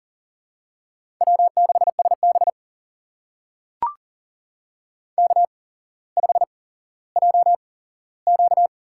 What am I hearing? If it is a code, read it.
Morse code, heard as a keyed tone, about 38 wpm, spaced as for 17 wpm, 700 Hz. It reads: W6SB K5JQ
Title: USA Call Signs - 4 Characters - 8x Character Spacing - 40wpm